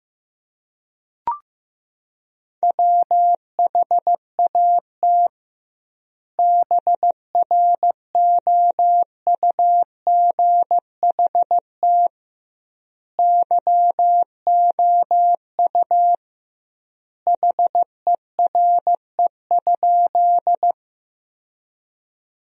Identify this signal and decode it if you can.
Morse code, heard as a keyed tone, about 16 wpm, 700 Hz. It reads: WHAT BROUGHT YOU HERE?
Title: 2x Word-Spacing - Sentences from Top 500 Words - 15wpm